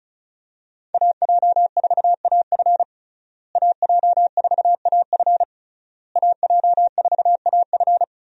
Morse code, heard as a keyed tone, about 36 wpm, 700 Hz. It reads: AJ4AF AJ4AF AJ4AF
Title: USA Call Signs - 5 Characters - Repeated 3x - 35wpm